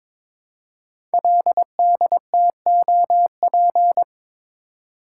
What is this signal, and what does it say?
Morse code, heard as a keyed tone, about 22 wpm, 700 Hz. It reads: LDTOP
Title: Code Groups - 5 Letters - 22wpm